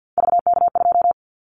Morse code, heard as a keyed tone, about 50 wpm, 700 Hz. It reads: VK2
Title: Callsign Prefixes for Top 100 DXCC - 50wpm